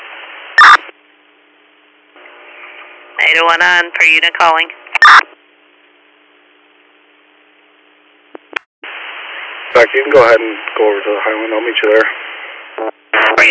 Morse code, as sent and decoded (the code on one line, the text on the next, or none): none